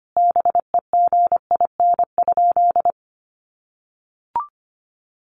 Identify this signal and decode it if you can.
Morse code, heard as a keyed tone, about 25 wpm, 700 Hz. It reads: BEGIN?
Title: Sentences from Top 300 Words 25wpm